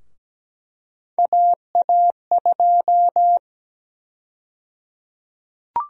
Morse code, heard as a keyed tone, about 17 wpm, 700 Hz. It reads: AA2
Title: USA Call Sign Prefixes - 17wpm